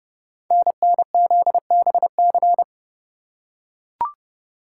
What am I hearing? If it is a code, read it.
Morse code, heard as a keyed tone, about 28 wpm, 700 Hz. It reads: NNZBC
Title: Code Groups - 5 Letters - 30wpm